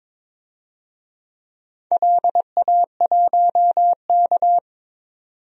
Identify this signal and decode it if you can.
Morse code, heard as a keyed tone, about 21 wpm, 700 Hz. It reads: LA1K
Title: DXCC-100 Call Signs - 4 Characters - 22wpm